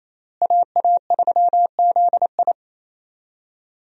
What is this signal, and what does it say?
Morse code, heard as a keyed tone, about 28 wpm, 700 Hz. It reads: AA3ZI
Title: USA Call Signs - 5 Characters - Repeated 3x - 28wpm